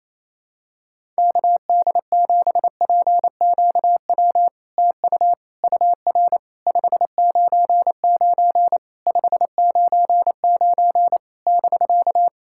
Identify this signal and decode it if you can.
Morse code, heard as a keyed tone, about 28 wpm, 700 Hz. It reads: KD7PQW TU UR 599 599 <BK>
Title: SOTA QSO Practice 28wpm